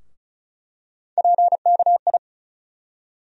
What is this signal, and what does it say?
Morse code, heard as a keyed tone, about 34 wpm, 700 Hz. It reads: PKI
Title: USA Call Sign Postfixes - 3 Letters 35wpm